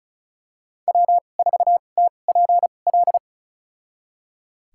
Morse code, heard as a keyed tone, about 34 wpm, spaced as for 25 wpm, 700 Hz. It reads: W4TPL
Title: USA Call Signs - 5 Characters - 2x Character Spacing - 35wpm